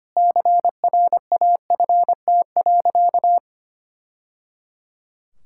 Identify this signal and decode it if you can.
Morse code, heard as a keyed tone, about 25 wpm, 700 Hz. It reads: CRAFT.
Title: Sentences from Top 1000 Words - Review Part 6 - 25wpm